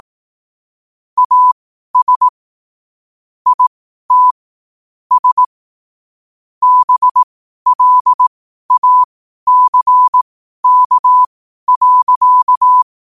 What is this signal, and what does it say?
Morse code, heard as a keyed tone, about 18 wpm, spaced as for 12 wpm, 1000 Hz. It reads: AS IT S BLACK.
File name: qotdfiles / 12wpm